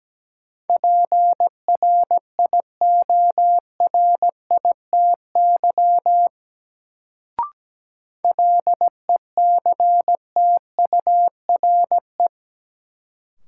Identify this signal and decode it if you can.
Morse code, heard as a keyed tone, about 17 wpm, 700 Hz. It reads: PRIORITY LECTURE
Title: Sentences from Top 1600 Words - New Word Drill Alt - 17wpm